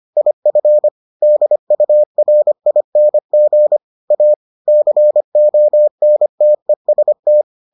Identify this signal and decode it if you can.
Morse code, heard as a keyed tone, about 25 wpm, 600 Hz. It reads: IF DURING A CONTEST